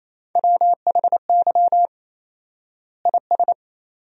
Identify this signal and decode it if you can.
Morse code, heard as a keyed tone, about 28 wpm, 700 Hz. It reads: WHY IS
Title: 4x Word-Spacing - Sentences from Top 200 Words - 28wpm